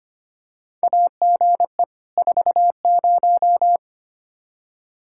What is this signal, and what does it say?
Morse code, heard as a keyed tone, about 25 wpm, 700 Hz. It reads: AGE 40
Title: K1USN SST QSO Practice 25wpm